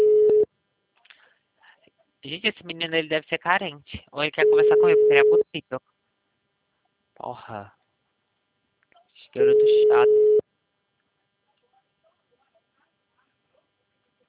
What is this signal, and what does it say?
Ringback tone (Europe, 425 Hz), with speech.
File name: out- -2016- - - .54181